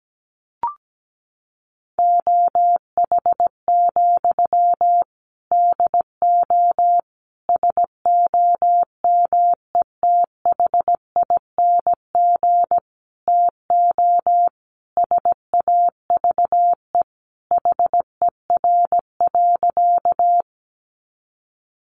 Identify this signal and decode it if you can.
Morse code, heard as a keyed tone, about 17 wpm, 700 Hz. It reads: OH, DO SOMETHING TO SAVE HER.
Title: Sentences from Top 1100 Words 17wpm